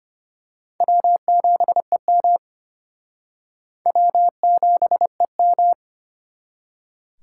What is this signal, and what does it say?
Morse code, heard as a keyed tone, about 29 wpm, 700 Hz. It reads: W7EM W7EM
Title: Speed Racing - USA 1x2 Call Signs 25wpm